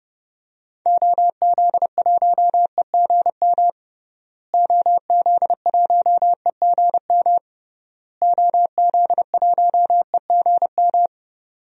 Morse code, heard as a keyed tone, about 32 wpm, 700 Hz. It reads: OZ1EGM OZ1EGM OZ1EGM
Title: DXCC-100 Call Signs - 6 Characters - Repeated 3x - 30wpm